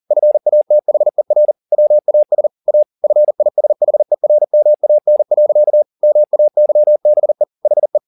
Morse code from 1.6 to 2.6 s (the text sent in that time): WAS